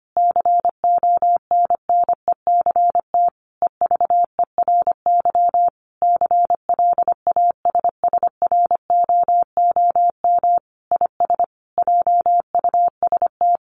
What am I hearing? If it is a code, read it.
Morse code, heard as a keyed tone, about 25 wpm, 700 Hz. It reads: CONNECT EVERY CLASSROOM IS JUST
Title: Sentences from Top 1500 Words 25wpm